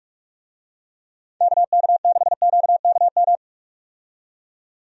Morse code, heard as a keyed tone, about 42 wpm, 700 Hz. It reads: KK6QKM
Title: USA Call Signs - 6 Characters - 45wpm